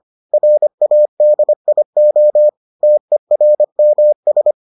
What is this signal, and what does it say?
Morse code, heard as a keyed tone, about 25 wpm, 600 Hz. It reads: RADIO TERMS